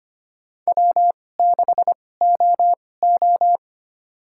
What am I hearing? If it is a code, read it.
Morse code, heard as a keyed tone, about 25 wpm, spaced as for 18 wpm, 700 Hz. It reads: W6OO
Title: USA Call Signs - 4 Characters - 2x Character Spacing - 25wpm